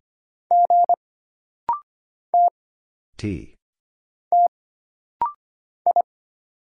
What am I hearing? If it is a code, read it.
Morse code, heard as a keyed tone, about 25 wpm, 700 Hz. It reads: G T T I
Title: Single Letter-Number - Rapid-Fire 25wpm